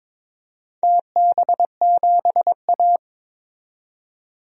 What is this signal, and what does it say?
Morse code, heard as a keyed tone, about 22 wpm, 700 Hz. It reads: TB7A
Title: DXCC-100 Call Signs - 4 Characters - 22wpm